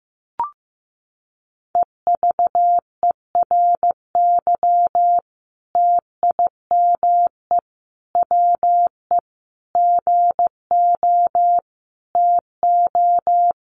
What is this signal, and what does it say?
Morse code, heard as a keyed tone, about 15 wpm, 700 Hz. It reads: EVERY TIME WE GO TO
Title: Sentences from Top 1300 Words 15wpm